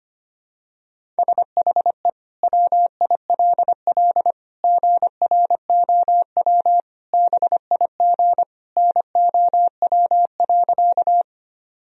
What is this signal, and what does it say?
Morse code, heard as a keyed tone, about 25 wpm, 700 Hz. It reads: SHE WILL GROW BIG NOW.
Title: Sentences from Top 1000 Words - Review Part 1 - 25wpm